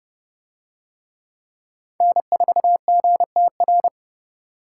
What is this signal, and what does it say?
Morse code, heard as a keyed tone, about 26 wpm, 700 Hz. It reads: N4GTR